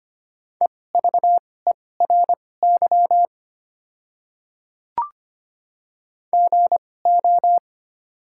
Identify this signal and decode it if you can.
Morse code, heard as a keyed tone, about 25 wpm, spaced as for 18 wpm, 700 Hz. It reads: EVERY GO
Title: 2x Character Spacing - Top 300 Words - 25wpm